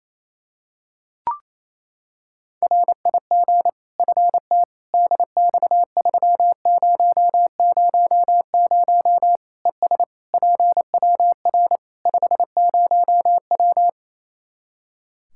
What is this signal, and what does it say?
Morse code, heard as a keyed tone, about 28 wpm, 700 Hz. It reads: RIG FT DX3000 ES PWR 50W
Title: Practice QSO - Exchange 2 - Info - 28wpm